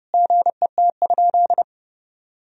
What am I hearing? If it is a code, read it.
Morse code, heard as a keyed tone, about 32 wpm, 700 Hz. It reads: GET?
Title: Sentences from Top 1000 Words - Review Part 3 - 30wpm